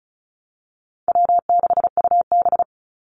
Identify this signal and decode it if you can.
Morse code, heard as a keyed tone, about 34 wpm, 700 Hz. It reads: W6UB
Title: NA SOTA Activator Callsigns 35wpm